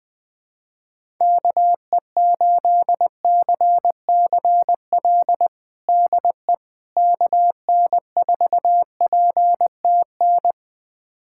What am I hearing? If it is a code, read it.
Morse code, heard as a keyed tone, about 20 wpm, 700 Hz. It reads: KE8CCL DE KN4PTN
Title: USA Callsign DE Callsign - 6 Characters - 20wpm